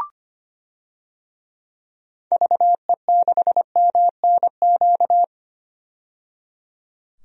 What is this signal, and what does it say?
Morse code, heard as a keyed tone, about 25 wpm, 690 Hz. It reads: VE6MNQ
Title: DXCC-100 Call Signs - 6 Characters - 25wpm